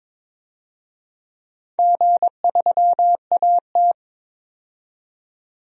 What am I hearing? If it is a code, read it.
Morse code, heard as a keyed tone, about 21 wpm, 700 Hz. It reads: G3AT
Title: DXCC-100 Call Signs - 4 Characters - 22wpm